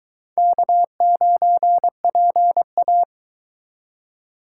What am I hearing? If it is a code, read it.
Morse code, heard as a keyed tone, about 22 wpm, 700 Hz. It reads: K9PA